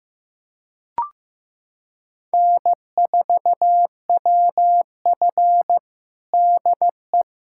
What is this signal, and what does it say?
Morse code, heard as a keyed tone, about 16 wpm, 700 Hz. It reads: N4WF DE
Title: USA Callsign DE Callsign - 4 Characters - 15wpm